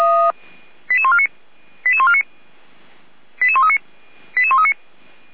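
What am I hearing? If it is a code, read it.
Five-tone selective call (ZVEI), heard as quick runs of five tones, tones of about 70 ms. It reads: 88149 88149 88139 88139
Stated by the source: ZVEI five-tone selective call received by SDR.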